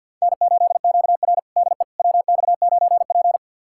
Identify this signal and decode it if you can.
Morse code, heard as a keyed tone, about 52 wpm, 700 Hz. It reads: N8QR DE WX9P
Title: USA Callsign DE Callsign - 4 Characters - 50wpm